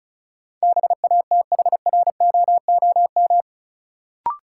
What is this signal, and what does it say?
Morse code, heard as a keyed tone, about 36 wpm, 700 Hz. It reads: BATHROOM
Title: Sets of 5 Words Encore - New Word Drill - 35wpm